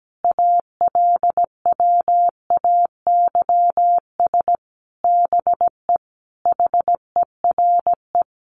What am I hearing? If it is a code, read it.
Morse code, heard as a keyed tone, about 17 wpm, 700 Hz. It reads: ALWAYS BE HERE
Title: Sentences from Top 1500 Words 17wpm